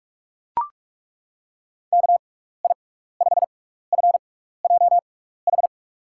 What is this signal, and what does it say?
Morse code, heard as a keyed tone, about 42 wpm, spaced as for 15 wpm, 700 Hz. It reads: KI5FJH
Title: USA Call Signs - 6 Characters - 6x Character Spacing - 45wpm